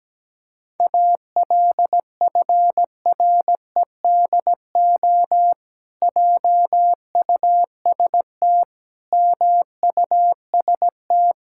Morse code, heard as a keyed tone, about 17 wpm, 700 Hz. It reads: ALFREDO JUST MUST
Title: Pangram Challenge v2 - 17wpm